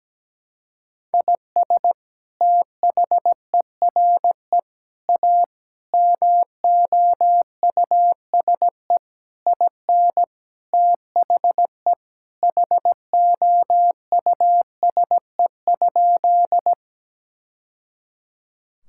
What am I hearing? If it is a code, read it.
Morse code, heard as a keyed tone, about 17 wpm, 700 Hz. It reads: IS THERE A MOUSE IN THE HOUSE?
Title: Sentences from Top 1000 Words - Review Part 5 - 17wpm